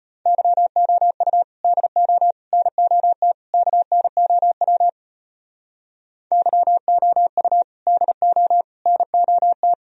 Morse code, heard as a keyed tone, about 30 wpm, 700 Hz. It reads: YOU DO NOT KNOW YOU DO NOT
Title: Speed-Racing - Sets of 4 Words 25wpm